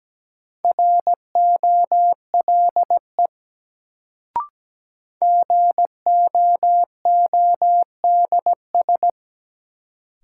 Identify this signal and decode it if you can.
Morse code, heard as a keyed tone, about 17 wpm, 700 Hz. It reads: ROLE GOODS
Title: Sentences from Top 1000 Words - Review Part 6 - New Word Drill - 17wpm